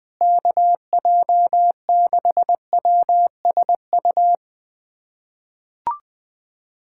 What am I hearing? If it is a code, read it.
Morse code, heard as a keyed tone, about 21 wpm, 700 Hz. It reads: KJ6WSU
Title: USA Callsign DE Callsign 20wpm